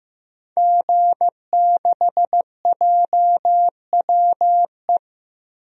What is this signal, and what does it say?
Morse code, heard as a keyed tone, about 15 wpm, 700 Hz. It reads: G6JWE
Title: DXCC-100 Call Signs - 5 Characters - Repeated 3x - 15wpm